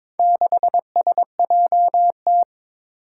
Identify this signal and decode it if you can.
Morse code, heard as a keyed tone, about 22 wpm, 700 Hz. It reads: 6SJT
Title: Four Challenging Letter-Number Combinations 22wpm